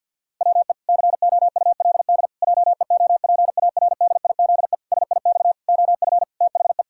Morse code, heard as a keyed tone, about 53 wpm, 700 Hz. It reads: WE COULD JEOPARDIZE SIX OF THE
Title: Pangram Challenge v2 - 50wpm